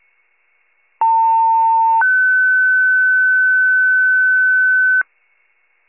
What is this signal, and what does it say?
Two-tone page, heard as a one-second tone, then a three-second tone, about 900 then 1500 Hz. It is heard from 1.0 to 5.0 s.